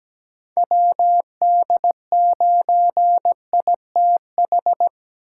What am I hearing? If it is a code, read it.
Morse code, heard as a keyed tone, about 17 wpm, 700 Hz. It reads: WD9ITH
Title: USA Call Signs - 8x Character Spacing - 17wpm